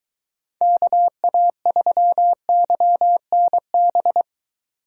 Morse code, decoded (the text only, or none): KA3YNB